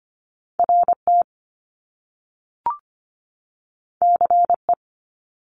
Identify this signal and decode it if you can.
Morse code, heard as a keyed tone, about 25 wpm, 700 Hz. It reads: RT CE
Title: Top Two Letter Combinations in Words 25wpm